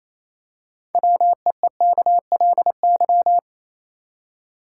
Morse code, heard as a keyed tone, about 28 wpm, 700 Hz. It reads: WEEKLY
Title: Six Letter Words - 28wpm